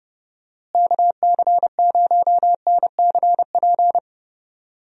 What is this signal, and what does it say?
Morse code, heard as a keyed tone, about 29 wpm, 700 Hz. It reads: KC0NCP
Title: USA Call Signs - 6 Characters - Repeated 3x - 30wpm